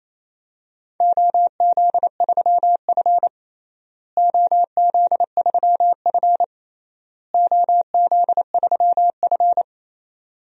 Morse code, heard as a keyed tone, about 28 wpm, 700 Hz. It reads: OZ3F OZ3F OZ3F
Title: DXCC-100 Call Signs - 4 Characters - Repeated 3x - 28wpm